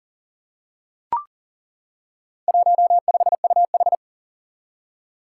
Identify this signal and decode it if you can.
Morse code, heard as a keyed tone, about 40 wpm, 700 Hz. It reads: 15UH
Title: Four Challenging Letter-Number Combinations 40wpm